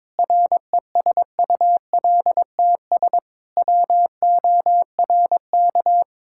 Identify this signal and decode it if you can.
Morse code, heard as a keyed tone, about 22 wpm, 700 Hz. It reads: RESULTS WORK
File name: practice / 22wpm / Sentences with 3-Word Phrases - 22wpm